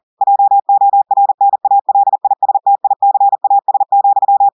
Morse code, heard as a keyed tone, about 40 wpm, 800 Hz. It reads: JORNALISTICAS,
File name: audios / portugais / portugais texte en clair 2 vitesse 40 mots minutes